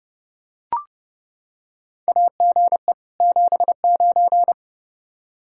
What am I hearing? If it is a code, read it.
Morse code, heard as a keyed tone, about 28 wpm, 700 Hz. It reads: AGE 79
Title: Practice QSO - Exchange 3 - More Info - 30wpm